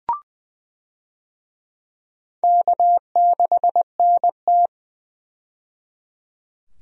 Morse code, heard as a keyed tone, about 20 wpm, 700 Hz. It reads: K6NT
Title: USA Call Signs - 4 Characters - 20wpm